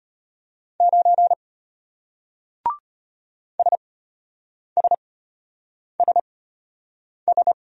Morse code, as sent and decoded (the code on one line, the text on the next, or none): ----. / ... ... ... ...
9 SSSS